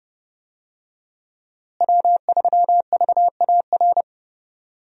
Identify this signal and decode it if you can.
Morse code, heard as a keyed tone, about 30 wpm, 700 Hz. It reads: W3VAR
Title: USA Call Signs - 5 Characters - Repeated 3x - 30wpm